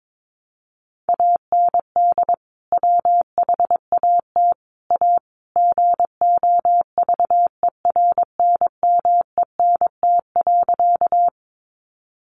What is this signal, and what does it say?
Morse code, heard as a keyed tone, about 22 wpm, 700 Hz. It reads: AND WHAT A GOVERNMENT.
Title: Sentences from Top 600 Words 22wpm